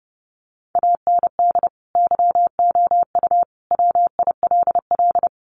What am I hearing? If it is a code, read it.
Morse code, heard as a keyed tone, about 30 wpm, 700 Hz. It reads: AND YOU WILL
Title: Sentences from Top 400 Words 30wpm